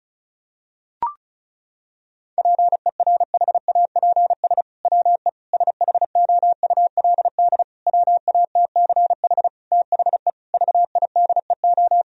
Morse code, heard as a keyed tone, about 34 wpm, 700 Hz. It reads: PERHAPS WE SHOULD WATCH THE VIDEO
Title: Sentences from Top 1000 Words - Review Part 4 - 35wpm